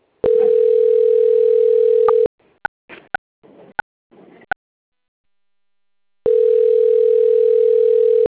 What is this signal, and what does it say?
Ringback tone (North America), 440 + 480 Hz.
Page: Duque de Caxias